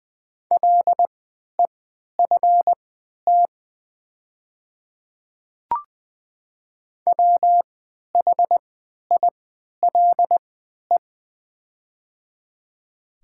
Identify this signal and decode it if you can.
Morse code, heard as a keyed tone, about 20 wpm, spaced as for 11 wpm, 700 Hz. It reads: LEFT WHILE
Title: 3x Character Spacing - Top 300 Words - 20wpm